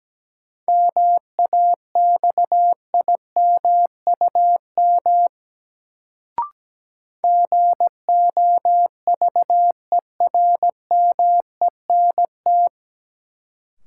Morse code, heard as a keyed tone, about 17 wpm, 700 Hz. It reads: MAXIMUM GOVERMENT
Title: Sentences from Top 1300 Words - New Word Drill Alt - 17wpm